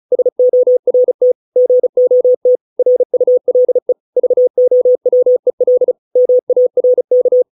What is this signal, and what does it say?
Morse code, heard as a keyed tone, about 35 wpm, 500 Hz. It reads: SORT GOT RULE VOWEL MARK